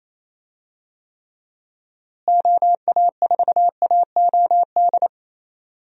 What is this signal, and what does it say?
Morse code, heard as a keyed tone, about 28 wpm, 700 Hz. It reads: OA4AOD
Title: USA Call Signs - 6 Characters - Repeated 2x - 28wpm